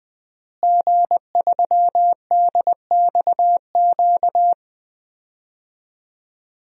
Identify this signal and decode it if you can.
Morse code, heard as a keyed tone, about 20 wpm, 700 Hz. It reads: G3DXQ
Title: DXCC-100 Call Signs - 5 Characters - Repeated 3x - 20wpm